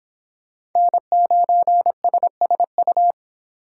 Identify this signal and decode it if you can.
Morse code, heard as a keyed tone, about 24 wpm, 700 Hz. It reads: N9SSU